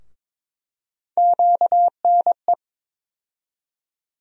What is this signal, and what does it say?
Morse code, heard as a keyed tone, about 22 wpm, 700 Hz. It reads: QNE
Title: USA Call Sign Postfixes - 3 Letters 22wpm